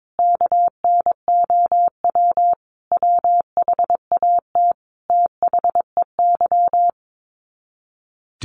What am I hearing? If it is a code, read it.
Morse code, heard as a keyed tone, about 22 wpm, 700 Hz. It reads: KNOW WHAT THEY
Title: Sets of 5 Words 22wpm